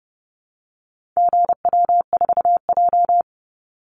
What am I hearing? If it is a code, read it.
Morse code, heard as a keyed tone, about 30 wpm, 700 Hz. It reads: GW4J
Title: Call Signs Repeated 2x - 30wpm